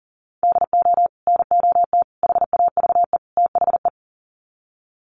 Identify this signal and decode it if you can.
Morse code, heard as a keyed tone, about 40 wpm, 700 Hz. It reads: DO NOT HAVE THE
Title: Sets of 4 Words 40wpm